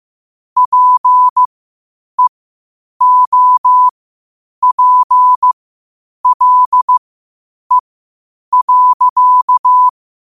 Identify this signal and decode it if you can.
Morse code, heard as a keyed tone, about 15 wpm, spaced as for 8 wpm, 1000 Hz. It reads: PEOPLE.
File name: qotdfiles / 8wpm